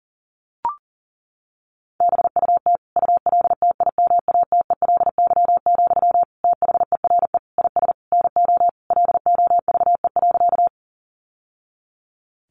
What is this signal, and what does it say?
Morse code, heard as a keyed tone, about 40 wpm, 700 Hz. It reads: BUT ULTIMATELY, THERE IS NO LOVE.
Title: Sentences from Top 1200 Words 40wpm